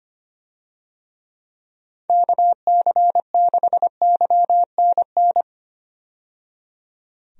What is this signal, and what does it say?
Morse code, heard as a keyed tone, about 25 wpm, 700 Hz. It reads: KC6YNN
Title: USA Call Signs - 6 Characters - 25wpm